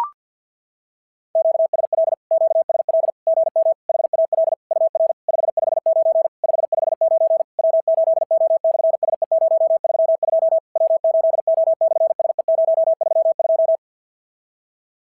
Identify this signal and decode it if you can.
Morse code, heard as a keyed tone, about 50 wpm, 650 Hz. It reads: QSL QSL GM HAL UR 559 559 W8O/SE032 W8O/SE032
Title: SOTA QSO Practice 50wpm